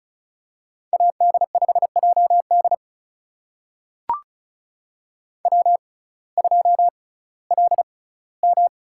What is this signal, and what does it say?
Morse code, heard as a keyed tone, about 36 wpm, spaced as for 19 wpm, 700 Hz. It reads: AD5JD W2LM
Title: USA Call Signs - 5 Characters - 6x Character Spacing - 35wpm